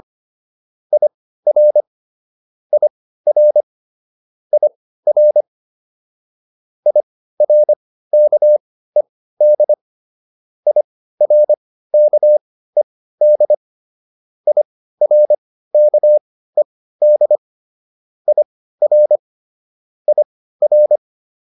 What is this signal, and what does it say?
Morse code, heard as a keyed tone, about 25 wpm, spaced as for 15 wpm, 600 Hz. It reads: IR IR IR IRKED IRKED IRKED IR IR